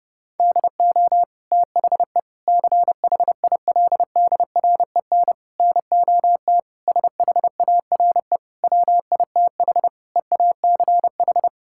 Morse code, heard as a keyed tone, about 32 wpm, 700 Hz. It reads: DO THE CHILDREN NOT SHARE WITH EACH
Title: Sentences from Top 1000 Words - Review Part 6 - 30wpm